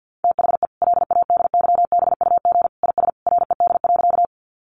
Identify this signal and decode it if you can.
Morse code, heard as a keyed tone, about 50 wpm, 700 Hz. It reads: THE LADYBUG IS RED.
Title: Additional Easy Phrases 50wpm